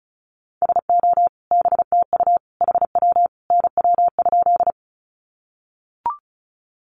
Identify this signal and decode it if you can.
Morse code, heard as a keyed tone, about 34 wpm, 700 Hz. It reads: SO BTU HW NW?
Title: Common QSO Phrases 35wpm